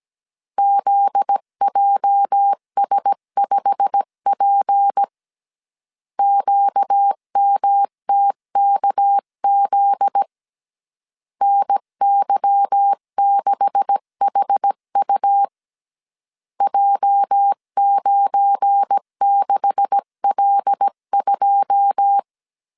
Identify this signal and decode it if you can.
Morse code, heard as a keyed tone, about 17 wpm, 800 Hz. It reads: ZJS5P QMTKZ NY6HU J96L2
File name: CW / DET / 1 9 1500C S B 16wpm 2856